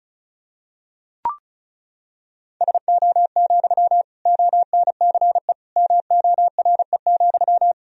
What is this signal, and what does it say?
Morse code, heard as a keyed tone, about 34 wpm, 700 Hz. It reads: SO, ONCE MORE,
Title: Sentences from Top 300 Words and Abbreviations 35wpm